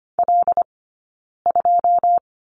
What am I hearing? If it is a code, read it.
Morse code, heard as a keyed tone, about 25 wpm, spaced as for 9 wpm, 690 Hz. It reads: L2
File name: Lesson 15d - Number 5 - Two Characters 25wpm